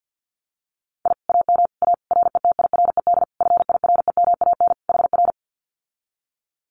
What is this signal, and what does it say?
Morse code, heard as a keyed tone, about 50 wpm, 700 Hz. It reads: I AM A RETIRED FIREMAN HR
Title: Common QSO Phrases Encore 50wpm